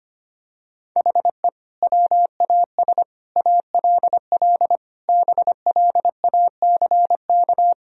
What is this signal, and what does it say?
Morse code, heard as a keyed tone, about 25 wpm, 700 Hz. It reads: HE WAS ALL BLACK
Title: Sentences from Top 1000 Words - Review Part 3 - 25wpm